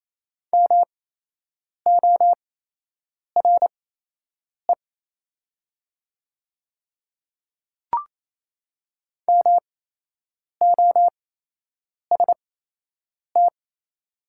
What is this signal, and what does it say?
Morse code, heard as a keyed tone, about 28 wpm, spaced as for 8 wpm, 700 Hz. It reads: MORE MOST
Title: 8x Character Spacing - Top 200 Words - 28wpm